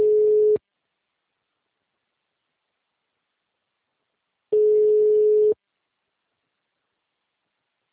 Ringback tone (Europe, 425 Hz).